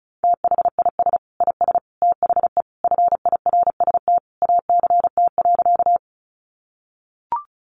Morse code, heard as a keyed tone, about 34 wpm, 700 Hz. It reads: THIS IS THE FIRST ACT.
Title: Sentences from Top 200 Words 35wpm